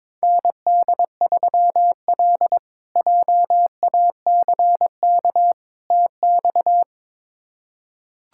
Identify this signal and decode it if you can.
Morse code, heard as a keyed tone, about 21 wpm, 700 Hz. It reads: ND3L JACK TX
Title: K1USN SST QSO Practice - 22wpm